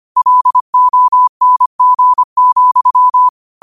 Morse code, heard as a keyed tone, about 25 wpm, 1000 Hz. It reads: LONG,
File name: 25wpm